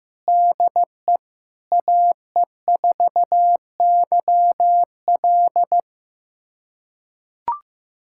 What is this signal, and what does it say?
Morse code, heard as a keyed tone, about 16 wpm, 700 Hz. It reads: DE AE4YL
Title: USA Callsign DE Callsign - 5 Characters - 15wpm